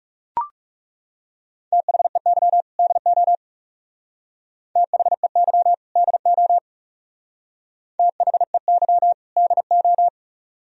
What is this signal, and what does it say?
Morse code, heard as a keyed tone, about 34 wpm, 700 Hz. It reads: THEY DO THEY DO THEY DO
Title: Speed-Racing - Sets of 2 Words 30wpm